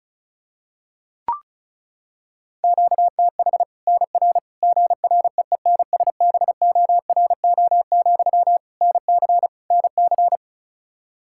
Morse code, heard as a keyed tone, about 36 wpm, 700 Hz. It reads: QTH NR GREENSBORO, NC NC
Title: Practice QSO - Exchange 1 - Beginning - 35wpm